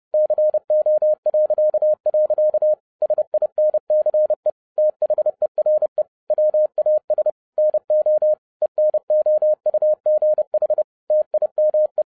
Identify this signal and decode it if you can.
Morse code, heard as a keyed tone, about 30 wpm, 610 Hz. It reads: CO.. SINCE THERE WAS NO ENOUGH TIME